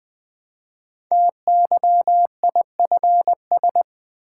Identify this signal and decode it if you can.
Morse code, heard as a keyed tone, about 20 wpm, 700 Hz. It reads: TYIFS
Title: Code Groups - 5 Letter:Numbers - 20wpm